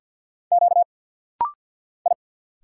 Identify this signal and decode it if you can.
Morse code, heard as a keyed tone, about 50 wpm, 700 Hz. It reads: Q I
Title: Single Letters - Rapid-Fire 50wpm